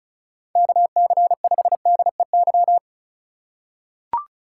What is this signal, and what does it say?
Morse code, heard as a keyed tone, about 36 wpm, 700 Hz. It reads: KC5DEY
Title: USA Call Signs - 6 Characters - 4x Character Spacing - 35wpm